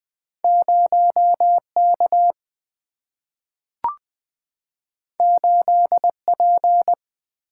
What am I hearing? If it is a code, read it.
Morse code, heard as a keyed tone, about 20 wpm, 700 Hz. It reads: 0 K 8 P
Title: Single-Letter Single-Number 20wpm